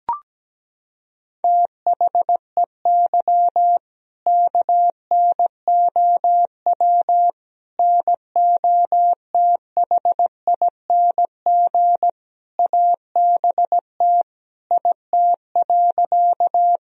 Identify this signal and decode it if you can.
Morse code, heard as a keyed tone, about 17 wpm, 700 Hz. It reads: THEY KNOW NOTHING ABT IT.
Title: Sentences from Top 500 Words and Abbreviations 17wpm